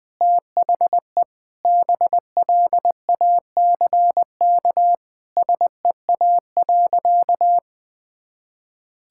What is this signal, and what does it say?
Morse code, heard as a keyed tone, about 20 wpm, 700 Hz. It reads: THE BLACK SEA.
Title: Sentences from Top 1000 Words - Review Part 5 - 20wpm